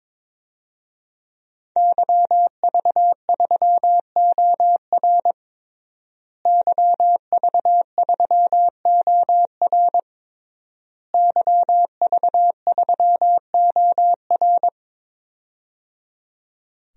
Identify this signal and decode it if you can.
Morse code, heard as a keyed tone, about 22 wpm, 700 Hz. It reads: YV3OR YV3OR YV3OR
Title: DXCC-100 Call Signs - 5 Characters - Repeated 3x - 22wpm